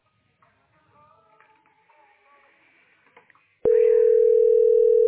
Ringback tone (North America), 440 + 480 Hz.
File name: 00bf2699-0ecc-4892-8bdd